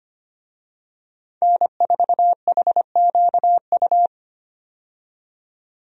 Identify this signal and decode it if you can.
Morse code, heard as a keyed tone, about 25 wpm, 690 Hz. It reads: N4HQU
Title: USA Call Signs - 5 Characters - 25wpm